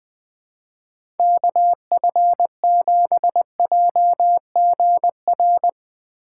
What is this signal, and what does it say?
Morse code, heard as a keyed tone, about 21 wpm, 700 Hz. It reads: KF7JGR